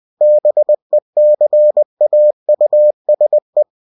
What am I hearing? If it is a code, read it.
Morse code, heard as a keyed tone, about 20 wpm, 600 Hz. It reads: BECAUSE
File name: cw-books / winnie-the-pooh / winnie-20wpm-page-0102